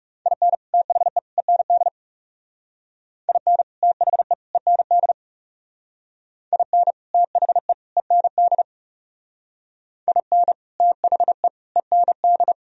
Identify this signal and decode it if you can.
Morse code, heard as a keyed tone, about 38 wpm, 700 Hz. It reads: IN THE END IN THE END IN THE END IN THE END